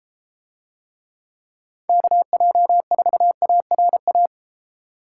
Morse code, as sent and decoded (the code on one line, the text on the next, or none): -.- .--- ....- .- .-. .-
KJ4ARA